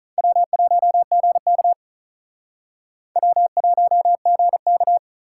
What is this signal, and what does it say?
Morse code, heard as a keyed tone, about 45 wpm, 700 Hz. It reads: W1GK W1GK